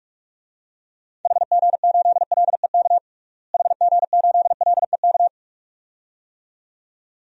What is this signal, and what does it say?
Morse code, heard as a keyed tone, about 43 wpm, 700 Hz. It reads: HG8LEK HG8LEK
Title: DXCC-100 Call Signs - 6 Characters - Repeated 2x - 45wpm